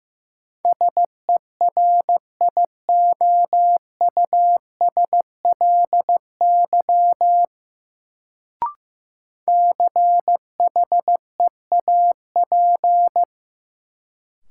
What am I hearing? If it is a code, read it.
Morse code, heard as a keyed tone, about 15 wpm, 700 Hz. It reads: SERIOUSLY CHEAP
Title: Sentences from Top 1200 Words - New Word Drill Alt - 15wpm